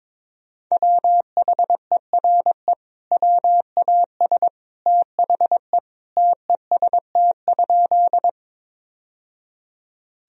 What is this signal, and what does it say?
Morse code, heard as a keyed tone, about 22 wpm, 700 Hz. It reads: WHERE WAS THE TEST?
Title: Sentences from Top 1000 Words - Review Part 3 - 22wpm